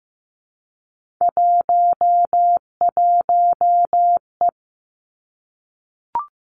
Morse code, heard as a keyed tone, about 15 wpm, 700 Hz. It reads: 11E
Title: Three Challenging Letter-Number Combinations 15wpm